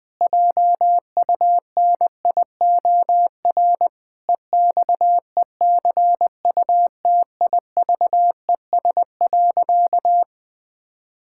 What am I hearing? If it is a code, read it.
Morse code, heard as a keyed tone, about 20 wpm, 700 Hz. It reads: JUNIOR EXECUTIVES.
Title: Pangram Challenge v2 - 20wpm